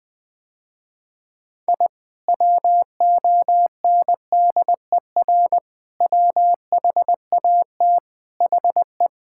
Morse code, heard as a keyed tone, about 20 wpm, 700 Hz. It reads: I WONDER WHAT HE
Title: Sentences with 2-Word Phrases - Spoken Beginning - 20wpm